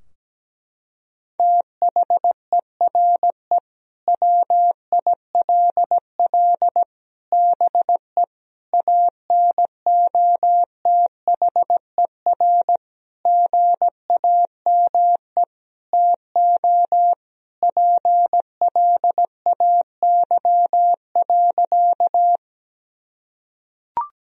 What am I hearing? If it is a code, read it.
Morse code, heard as a keyed tone, about 17 wpm, 700 Hz. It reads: THERE WILL BE ANOTHER GAME TO PLAY.
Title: Sentences from Top 1000 Words - Review Part 3 - 17wpm